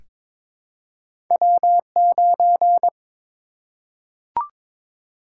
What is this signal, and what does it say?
Morse code, heard as a keyed tone, about 22 wpm, 700 Hz. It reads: W9
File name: practice / 22wpm / Two Letter-Number Combinations - 22wpm